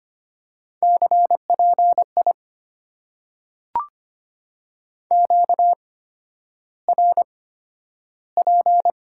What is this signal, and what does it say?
Morse code, heard as a keyed tone, about 25 wpm, spaced as for 11 wpm, 700 Hz. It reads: CPI QRP
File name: Top 90 QSO Elements - 8x Character Spacing - 25wpm